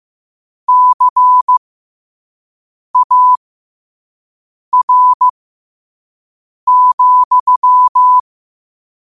Morse code, heard as a keyed tone, about 15 wpm, spaced as for 5 wpm, 1000 Hz. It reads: CAR,
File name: qotdfiles / 5wpm